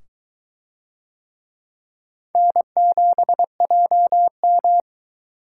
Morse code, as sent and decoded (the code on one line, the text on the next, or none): -. --... .--- --
N7JM